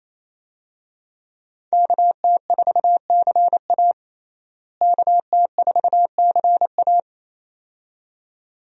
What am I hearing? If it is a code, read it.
Morse code, heard as a keyed tone, about 28 wpm, 700 Hz. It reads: KT4CA KT4CA
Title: USA Call Signs - 5 Characters - Repeated 2x - 28wpm